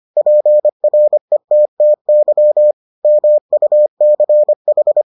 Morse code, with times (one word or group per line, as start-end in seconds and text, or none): 0.2-2.7 PRETTY
3.0-5.0 MUCH